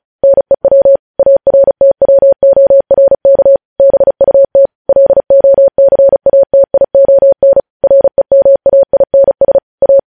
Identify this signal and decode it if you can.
Morse code, heard as a keyed tone, about 36 wpm, 550 Hz. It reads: NEW ARTWORK BUT LOCATION REMAINS A
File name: podcast / News headlines-1x-35x35